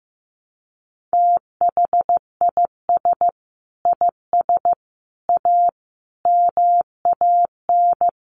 Morse code, heard as a keyed tone, about 15 wpm, 700 Hz. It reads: THIS IS A MAN
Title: Sets of 5 Words Encore2 15wpm